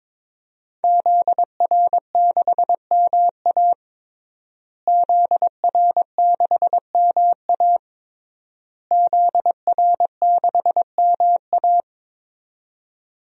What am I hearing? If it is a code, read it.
Morse code, heard as a keyed tone, about 22 wpm, 700 Hz. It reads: ZR6MA ZR6MA ZR6MA
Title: DXCC-100 Call Signs - 5 Characters - Repeated 3x - 22wpm